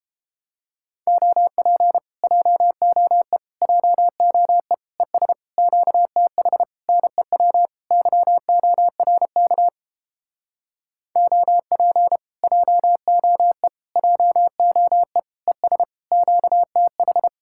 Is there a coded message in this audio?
Yes: Morse code from 1.1 to 17.3 s, heard as a keyed tone, about 31 wpm, 700 Hz. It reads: OP JOE JOE ES QTH NEW YORK OP JOE JOE ES QTH